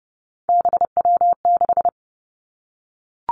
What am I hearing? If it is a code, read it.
Morse code, heard as a keyed tone, about 32 wpm, 700 Hz. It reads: BW6
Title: Three Challenging Letter-Number Combinations 30wpm